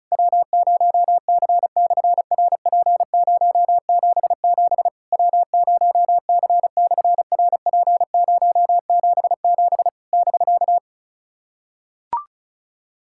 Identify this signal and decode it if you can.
Morse code, heard as a keyed tone, about 33 wpm, 700 Hz. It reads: W0C/RP077 W0C/RP077 <BK>
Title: SOTA QSO Practice 35wpm